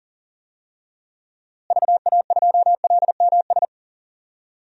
Morse code, heard as a keyed tone, about 40 wpm, 700 Hz. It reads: VA2LMS
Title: DXCC-100 Call Signs - 6 Characters - Repeated 2x - 40wpm